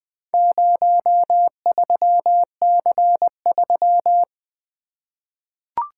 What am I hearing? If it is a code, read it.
Morse code, heard as a keyed tone, about 20 wpm, 700 Hz. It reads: 03C3
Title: Four Letter-Number Combinations - 20wpm